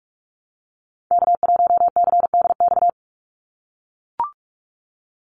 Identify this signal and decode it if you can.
Morse code, heard as a keyed tone, about 44 wpm, 700 Hz. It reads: K1CDX
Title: Call Signs - 45wpm